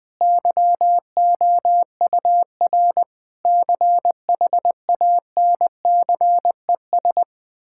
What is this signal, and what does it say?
Morse code, heard as a keyed tone, about 20 wpm, 700 Hz. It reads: YOUR CHANCES
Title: Sentences with 8-Word Phrases - Spoken Beginning - 20wpm